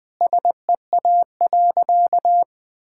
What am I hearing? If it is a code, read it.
Morse code, heard as a keyed tone, about 20 wpm, 700 Hz. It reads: SEA.
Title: Sentences from Top 1000 Words - Review Part 5 - 20wpm